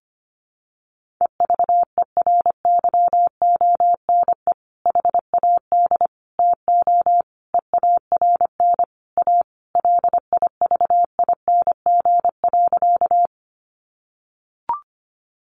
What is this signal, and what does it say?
Morse code, heard as a keyed tone, about 25 wpm, 700 Hz. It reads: EVERYONE HAD TO EARN A LIVING.
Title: Sentences from Top 1500 Words 25wpm